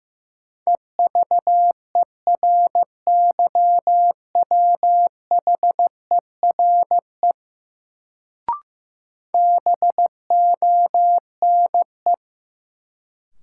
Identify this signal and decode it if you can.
Morse code, heard as a keyed tone, about 15 wpm, 700 Hz. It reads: EVERYWHERE BONE
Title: Sentences from Top 1600 Words - New Word Drill Alt - 15wpm